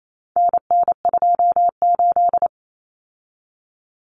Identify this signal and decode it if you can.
Morse code, heard as a keyed tone, about 28 wpm, 700 Hz. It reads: NN28
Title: Common Cut Numbers - 4-Digit 28wpm